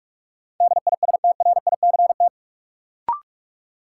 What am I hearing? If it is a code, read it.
Morse code, heard as a keyed tone, about 46 wpm, 700 Hz. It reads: DISTRICT
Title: Sentences from Top 1600 Words - New Word Drill Alt - 45wpm